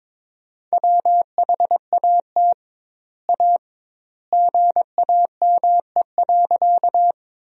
Morse code, heard as a keyed tone, about 22 wpm, 700 Hz. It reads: WHAT A GAME.
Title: 2x Word-Spacing - Sentences from Top 500 Words - 22wpm